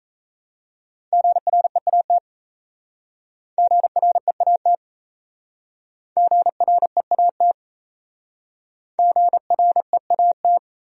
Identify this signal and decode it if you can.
Morse code, heard as a keyed tone, about 35 wpm, 700 Hz. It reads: GREAT GREAT GREAT GREAT
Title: Speed-Racing Top 200 Words 28wpm